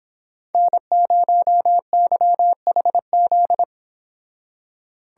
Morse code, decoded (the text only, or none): N0YHZ